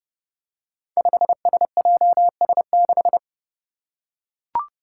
Morse code, heard as a keyed tone, about 30 wpm, 700 Hz. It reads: 5SJS6
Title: Five Challenging Letter-Number Combinations 30wpm